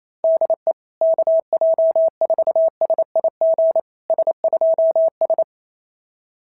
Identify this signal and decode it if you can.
Morse code, heard as a keyed tone, about 29 wpm, 650 Hz. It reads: DE KJ4SIG S2S
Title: SOTA QSO Practice 28wpm